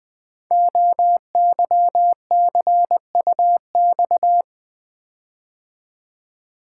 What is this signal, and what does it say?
Morse code, heard as a keyed tone, about 20 wpm, 700 Hz. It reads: OYCUX
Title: Code Groups - 5 Letter:Numbers - 20wpm